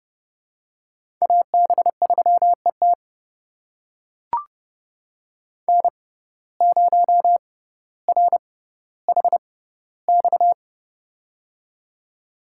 Morse code, heard as a keyed tone, about 30 wpm, spaced as for 15 wpm, 700 Hz. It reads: AB3ET N0RHX
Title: USA Call Signs - 5 Characters - 6x Character Spacing - 30wpm